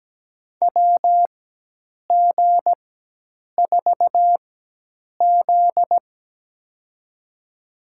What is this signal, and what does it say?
Morse code, heard as a keyed tone, about 17 wpm, spaced as for 8 wpm, 700 Hz. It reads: WG4Z
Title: USA Call Signs - 4 Characters - 4x Character Spacing - 17wpm